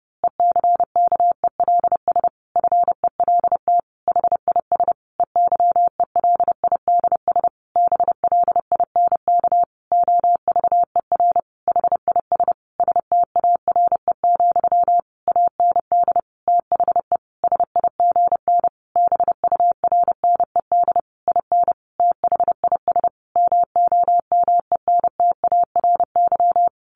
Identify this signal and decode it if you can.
Morse code, heard as a keyed tone, about 30 wpm, 700 Hz. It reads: ECKELS FELT HIS EYELIDS BLINK OVER HIS STARE, AND THE SIGN BURNED IN THIS MOMENTARY
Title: A Sound Of Thunder 30wpm